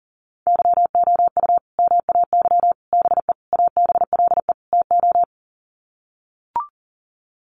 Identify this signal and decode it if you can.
Morse code, heard as a keyed tone, about 40 wpm, 700 Hz. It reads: YOU MAY BE ABLE TO
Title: Sets of 5 Words Encore 40wpm